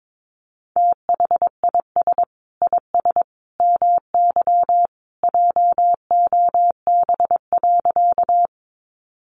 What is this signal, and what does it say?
Morse code, heard as a keyed tone, about 22 wpm, 700 Hz. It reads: THIS IS MY JOB.
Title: Sentences from Top 600 Words 22wpm